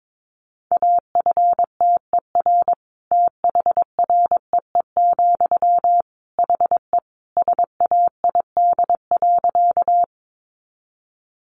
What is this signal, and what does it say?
Morse code, heard as a keyed tone, about 22 wpm, 700 Hz. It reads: AFTER THREE, HE SAID.
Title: Sentences from Top 200 Words 22wpm